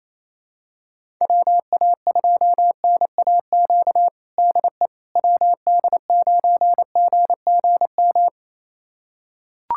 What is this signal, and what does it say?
Morse code, heard as a keyed tone, about 28 wpm, 700 Hz. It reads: WA2NAQ DE WD9GGM
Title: USA Callsign DE Callsign - 6 Characters - 28wpm